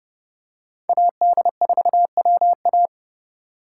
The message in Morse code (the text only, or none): AD4WA